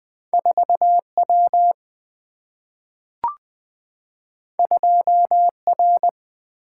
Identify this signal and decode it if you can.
Morse code, heard as a keyed tone, about 21 wpm, 700 Hz. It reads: 4W 2R
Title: Single-Letter Single-Number 20wpm